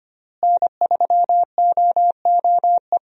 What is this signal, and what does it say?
Morse code, heard as a keyed tone, about 25 wpm, 700 Hz. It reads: N3OOE
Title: USA Call Signs - 5 Characters - Repeated 2x - 25wpm